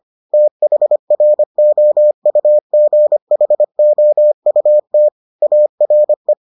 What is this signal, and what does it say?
Morse code, heard as a keyed tone, about 25 wpm, 600 Hz. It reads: THROUGHOUT ARE